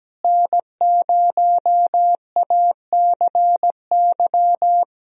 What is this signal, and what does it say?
Morse code, heard as a keyed tone, about 17 wpm, 700 Hz. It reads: N0ACY
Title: USA Call Signs - 5 Characters - Repeated 2x - 17wpm